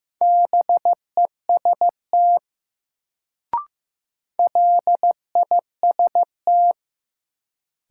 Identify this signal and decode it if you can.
Morse code, heard as a keyed tone, about 15 wpm, 700 Hz. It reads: BEST LIST
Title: Sentences from Top 400 Words - New Word Drill - 15wpm